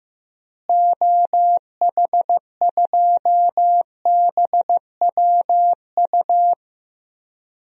Morse code, heard as a keyed tone, about 15 wpm, 700 Hz. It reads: OH2BWU
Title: DXCC-100 Call Signs - 6 Characters - Repeated 2x - 15wpm